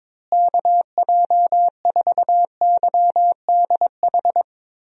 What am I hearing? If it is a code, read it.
Morse code, heard as a keyed tone, about 21 wpm, 700 Hz. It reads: KJ4YDH
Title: USA Call Signs - 6 Characters - Repeated 2x - 22wpm